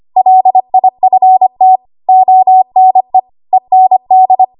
Morse code, heard as a keyed tone, about 25 wpm, 750 Hz. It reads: LIFT ONE END
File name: Archive / 25 WPM / 25WPM